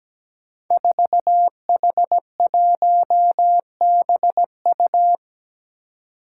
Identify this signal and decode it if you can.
Morse code, heard as a keyed tone, about 17 wpm, 700 Hz. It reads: 4H1BU
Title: Five Challenging Letter-Number Combinations 17wpm